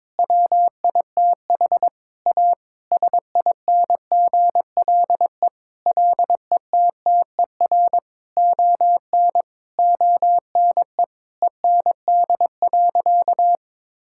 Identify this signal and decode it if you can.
Morse code, heard as a keyed tone, about 22 wpm, 700 Hz. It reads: WITH A SINGLE LETTER ON ONE END.
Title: Sentences from Top 1600 Words 22wpm